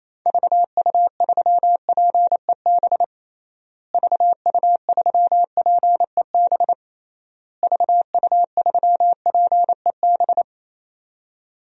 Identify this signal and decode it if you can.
Morse code, heard as a keyed tone, about 28 wpm, 700 Hz. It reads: VU3PEB VU3PEB VU3PEB
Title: DXCC-100 Call Signs - 6 Characters - Repeated 3x - 28wpm